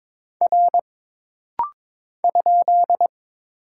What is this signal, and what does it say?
Morse code, heard as a keyed tone, about 22 wpm, 700 Hz. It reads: R ?
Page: This rapid-fire timing is geared to push your ICR (Instant Character Recognition) skills.